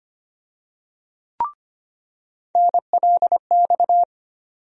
Morse code, heard as a keyed tone, about 25 wpm, 700 Hz. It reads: NLX
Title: USA Call Sign Postfixes - 3 Letters 25wpm